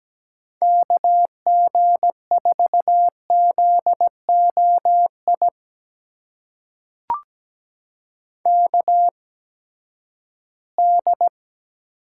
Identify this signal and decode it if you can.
Morse code, heard as a keyed tone, about 17 wpm, spaced as for 12 wpm, 700 Hz. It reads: KG4ZOI KD
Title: USA Call Signs - 6 Characters - 8x Character Spacing - 17wpm